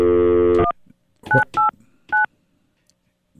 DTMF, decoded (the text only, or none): 1989